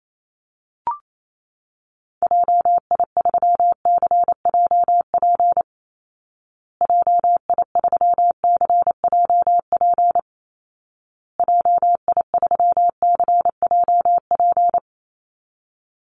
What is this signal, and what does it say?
Morse code, heard as a keyed tone, about 28 wpm, 700 Hz. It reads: JI3CJP JI3CJP JI3CJP
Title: Call Signs Repeated 2x - 28wpm